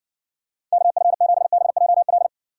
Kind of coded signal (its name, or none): Morse code